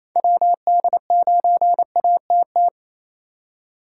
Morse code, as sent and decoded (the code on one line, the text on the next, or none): .-- -.. ----. .- - -
WD9ATT